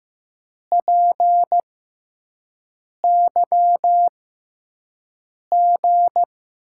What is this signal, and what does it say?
Morse code, heard as a keyed tone, about 15 wpm, spaced as for 5 wpm, 700 Hz. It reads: PYG
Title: USA Call Signs - 6 Characters - 6x Character Spacing - 15wpm